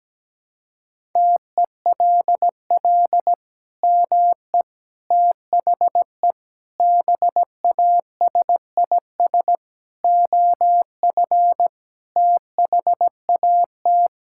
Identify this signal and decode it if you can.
Morse code, heard as a keyed tone, about 17 wpm, 700 Hz. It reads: TELL ME THE BASIS OF THAT
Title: Sentences from Top 1000 Words - Review Part 5 - 17wpm